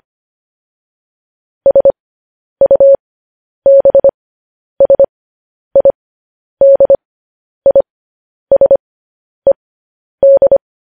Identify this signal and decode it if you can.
Morse code, heard as a keyed tone, about 25 wpm, spaced as for 10 wpm, 550 Hz. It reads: SUBSIDISED
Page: News Headlines in Morse Code at 10 WPM